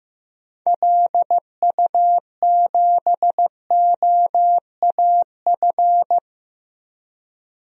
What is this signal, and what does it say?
Morse code, heard as a keyed tone, about 15 wpm, 700 Hz. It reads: LU7OAF
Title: DXCC-100 Call Signs - 6 Characters - Repeated 2x - 15wpm